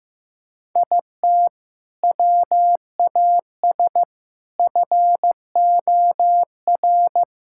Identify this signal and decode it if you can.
Morse code, heard as a keyed tone, about 16 wpm, 700 Hz. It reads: IT WAS FOR
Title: Sentences from Top 1000 Words - Review Part 6 - 15wpm